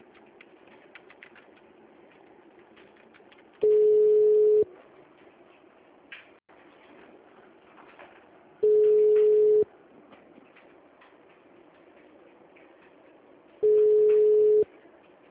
Ringback tone (Europe, 425 Hz).